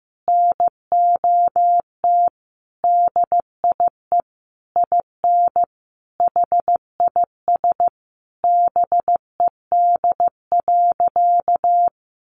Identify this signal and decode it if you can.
Morse code, heard as a keyed tone, about 15 wpm, 700 Hz. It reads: NOT DIE IN HIS BED.
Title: Sentences from Top 1300 Words 15wpm